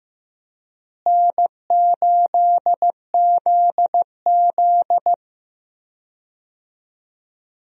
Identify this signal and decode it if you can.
Morse code, heard as a keyed tone, about 15 wpm, 700 Hz. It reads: N8ZZ
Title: USA Call Signs - 4 Characters - Repeated 2x - 15wpm